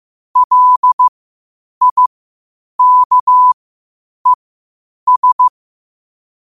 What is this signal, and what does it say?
Morse code, heard as a keyed tone, about 15 wpm, spaced as for 8 wpm, 1000 Hz. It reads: LIKES
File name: qotdfiles / 8wpm